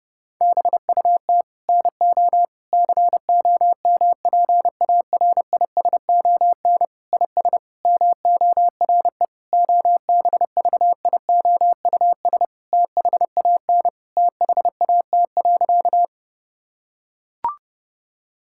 Morse code, heard as a keyed tone, about 30 wpm, 700 Hz. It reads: BUT NO COMPARISON IS MORE OBVIOUS THAN THAT.
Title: Sentences from Top 1600 Words 30wpm